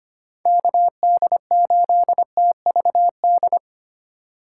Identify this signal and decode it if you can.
Morse code, heard as a keyed tone, about 25 wpm, 700 Hz. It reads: KD8TVD
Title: USA Call Signs - 6 Characters - 3x Character Spacing - 25wpm